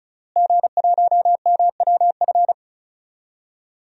Morse code, heard as a keyed tone, about 36 wpm, 700 Hz. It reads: G1MWF
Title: DXCC-100 Call Signs - 5 Characters - Repeated 3x - 35wpm